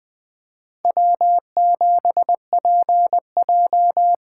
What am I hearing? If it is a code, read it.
Morse code, heard as a keyed tone, about 20 wpm, 700 Hz. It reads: W7PJ
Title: USA Call Signs - 4 Characters - 4x Character Spacing - 20wpm